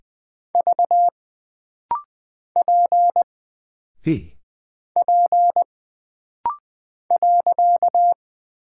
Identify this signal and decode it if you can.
Morse code, heard as a keyed tone, about 20 wpm, 700 Hz. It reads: V P P .
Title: Single Character (Letter:Number:Punctuation:Prosign) - Rapid-Fire - 20wpm